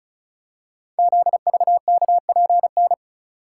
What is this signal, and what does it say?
Morse code, heard as a keyed tone, about 36 wpm, 690 Hz. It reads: ZVKPN
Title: Code Groups - 5 Letters - 35wpm